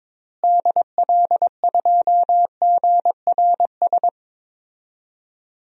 Morse code, heard as a keyed tone, about 22 wpm, 700 Hz. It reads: DL2GRS
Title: DXCC-100 Call Signs - 6 Characters - Repeated 3x - 22wpm